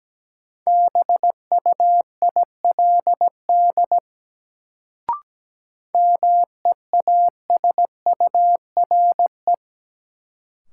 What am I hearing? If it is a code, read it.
Morse code, heard as a keyed tone, about 17 wpm, 700 Hz. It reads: BUILD MEASURE
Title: Sentences from Top 1000 Words - Review Part 1 - New Word Drill - 17wpm